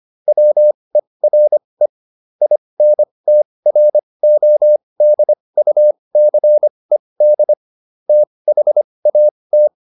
Morse code, heard as a keyed tone, about 25 wpm, spaced as for 20 wpm, 600 Hz. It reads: WERE INTRODUCED THAT
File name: SS215 20